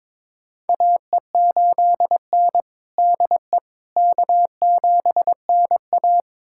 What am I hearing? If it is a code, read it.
Morse code, heard as a keyed tone, about 22 wpm, 700 Hz. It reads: AE8N DE K7NA
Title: USA Callsign DE Callsign - 4 Characters - 22wpm